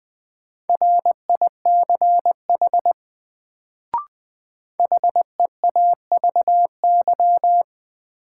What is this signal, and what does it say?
Morse code, heard as a keyed tone, about 20 wpm, 700 Hz. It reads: RICH HEAVY
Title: Sentences from Top 1100 Words - New Word Drill Alt - 20wpm